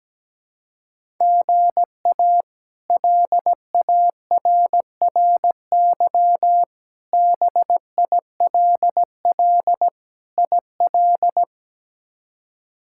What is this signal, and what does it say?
Morse code, heard as a keyed tone, about 17 wpm, 700 Hz. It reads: GA LARRY BILL IL
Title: K1USN SST QSO Practice - 17wpm